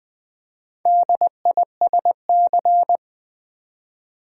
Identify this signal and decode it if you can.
Morse code, heard as a keyed tone, about 20 wpm, 700 Hz. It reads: DISC
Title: Build A Word - 2 Letters At A Time - 8-Letter Words - 20wpm